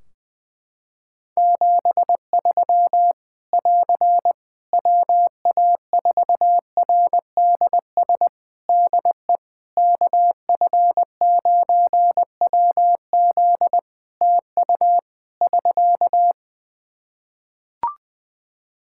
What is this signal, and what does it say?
Morse code, heard as a keyed tone, about 20 wpm, 700 Hz. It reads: 73 <AR> WA4RDS DE KF9WZ TU <SK>
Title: K1USN SST QSO Practice 20wpm